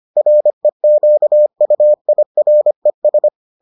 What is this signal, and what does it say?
Morse code, heard as a keyed tone, about 25 wpm, 600 Hz. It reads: REQUIRES